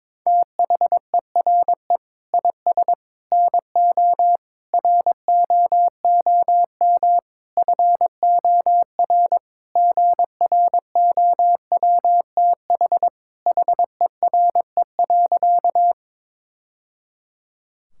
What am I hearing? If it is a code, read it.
Morse code, heard as a keyed tone, about 22 wpm, 700 Hz. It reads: THERE IS NO ROOM FOR GROWTH HERE.
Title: Sentences from Top 1000 Words - Review Part 5 - 22wpm